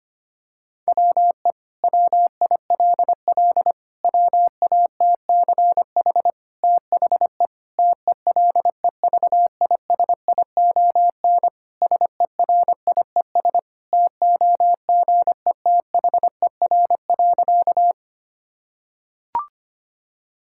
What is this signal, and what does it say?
Morse code, heard as a keyed tone, about 25 wpm, 700 Hz. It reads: WE WILL WATCH THE TELEVISION SERIES TOGETHER.
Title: Sentences from Top 1000 Words - Review Part 5 - 25wpm